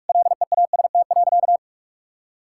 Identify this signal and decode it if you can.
Morse code, heard as a keyed tone, about 46 wpm, 700 Hz. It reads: LEAST.
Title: Sentences from Top 1000 Words - Review Part 5 - 45wpm